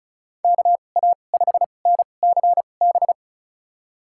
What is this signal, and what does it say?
Morse code, heard as a keyed tone, about 34 wpm, spaced as for 25 wpm, 700 Hz. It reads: KA5NCB
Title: USA Call Signs - 6 Characters - 2x Character Spacing - 35wpm